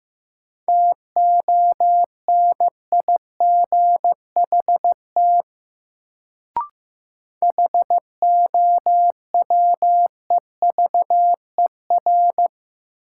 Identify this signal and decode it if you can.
Morse code, heard as a keyed tone, about 15 wpm, 700 Hz. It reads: TONIGHT HOWEVER
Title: Sentences from Top 600 Words - New Word Drill - 15wpm